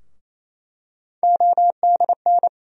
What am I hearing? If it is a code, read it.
Morse code, heard as a keyed tone, about 28 wpm, 700 Hz. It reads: ODN
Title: USA Call Sign Postfixes - 3 Letters 28wpm